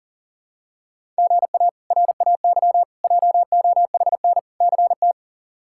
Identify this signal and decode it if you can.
Morse code, heard as a keyed tone, about 40 wpm, 700 Hz. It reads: GA RAY JOHN CT
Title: K1USN SST QSO Practice - 40wpm